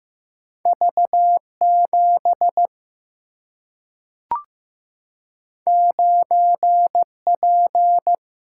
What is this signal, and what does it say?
Morse code, heard as a keyed tone, about 16 wpm, 700 Hz. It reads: V7 9P
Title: Single-Letter Single-Number 15wpm